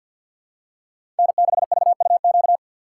Morse code, heard as a keyed tone, about 47 wpm, 700 Hz. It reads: N6VUQ
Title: USA Call Signs - 5 Characters - 3x Character Spacing - 50wpm